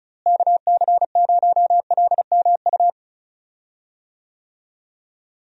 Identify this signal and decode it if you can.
Morse code, heard as a keyed tone, about 34 wpm, 700 Hz. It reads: KC0LMU
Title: USA Call Signs - 6 Characters - Repeated 3x - 35wpm